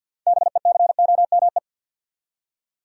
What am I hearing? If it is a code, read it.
Morse code, heard as a keyed tone, about 49 wpm, 700 Hz. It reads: BECOME
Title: Six Letter Words - 50wpm